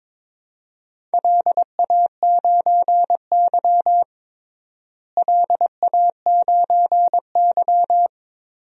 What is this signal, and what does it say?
Morse code, heard as a keyed tone, about 22 wpm, 700 Hz. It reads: LA9Y LA9Y
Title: DXCC-100 Call Signs - 4 Characters - Repeated 3x - 22wpm